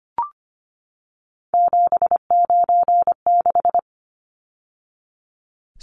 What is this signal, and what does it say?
Morse code, heard as a keyed tone, about 25 wpm, 690 Hz. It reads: 796
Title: Numbers - 3-Digit 25wpm